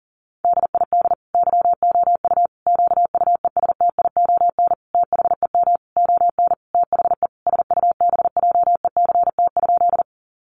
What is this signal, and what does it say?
Morse code, heard as a keyed tone, about 37 wpm, 700 Hz. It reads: DID YOU QUESTION THEM ON THE SUBJECT?
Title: Sentences from Top 800 Words 40wpm